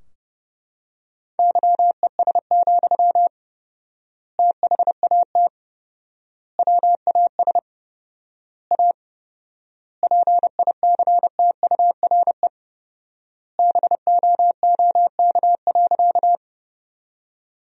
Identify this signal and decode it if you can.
Morse code, heard as a keyed tone, about 30 wpm, 700 Hz. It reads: YES, THAT WAS A PICTURE BOOK.
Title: 4x Word-Spacing - Sentences from Top 500 Words - 30wpm